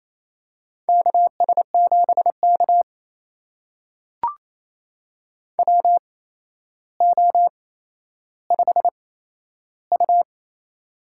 Morse code, heard as a keyed tone, about 28 wpm, spaced as for 12 wpm, 700 Hz. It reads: KS7K WO5U
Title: USA Call Signs - 4 Characters - 8x Character Spacing - 28wpm